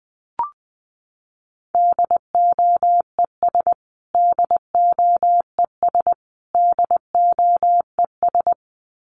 Morse code, heard as a keyed tone, about 20 wpm, 700 Hz. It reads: DOES DOES DOES
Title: Top 500 Words Repeated 20wpm